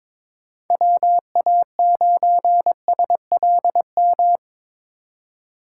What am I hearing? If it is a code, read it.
Morse code, heard as a keyed tone, about 22 wpm, 700 Hz. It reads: WA9SLM
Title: USA Call Signs - 2x Character Spacing - 22wpm